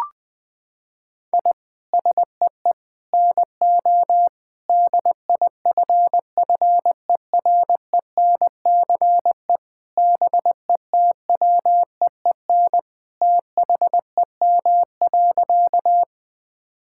Morse code, heard as a keyed tone, about 21 wpm, 700 Hz. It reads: I SEE NO DIFFERENCE BETWEEN THEM.
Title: Sentences from Top 1000 Words - Review Part 5 - 20wpm